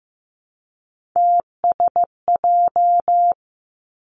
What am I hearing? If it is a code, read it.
Morse code, heard as a keyed tone, about 15 wpm, 700 Hz. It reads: TSJ
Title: Three Challenging Letter-Number Combinations 15wpm